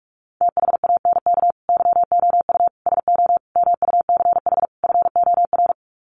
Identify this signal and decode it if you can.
Morse code, heard as a keyed tone, about 44 wpm, 700 Hz. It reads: THANK YOU SO MUCH FOR
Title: Sets of 5 Words 45wpm